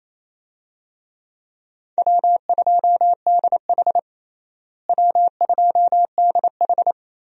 Morse code, heard as a keyed tone, about 28 wpm, 700 Hz. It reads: W2DH W2DH
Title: USA Call Signs - 4 Characters - Repeated 3x - 28wpm